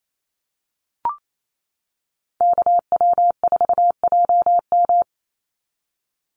Morse code, heard as a keyed tone, about 28 wpm, 700 Hz. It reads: KW4JM
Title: NA SOTA Chaser Callsigns 28wpm